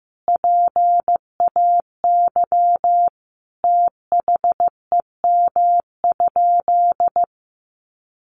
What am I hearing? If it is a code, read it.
Morse code, heard as a keyed tone, about 15 wpm, 700 Hz. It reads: PAY THEM?
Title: Sentences from Top 700 Words 15wpm